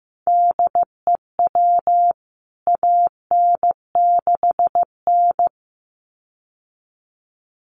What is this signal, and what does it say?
Morse code, heard as a keyed tone, about 15 wpm, 700 Hz. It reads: DEW AN6N
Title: CWT Exchange 15wpm